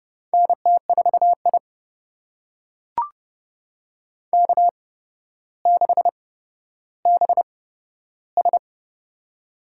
Morse code, heard as a keyed tone, about 30 wpm, spaced as for 13 wpm, 700 Hz. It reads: NT4I K6BS
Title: USA Call Signs - 4 Characters - 8x Character Spacing - 30wpm